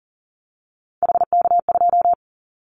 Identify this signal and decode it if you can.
Morse code, heard as a keyed tone, about 40 wpm, 700 Hz. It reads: HK2
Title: Callsign Prefixes for Top 100 DXCC - 40wpm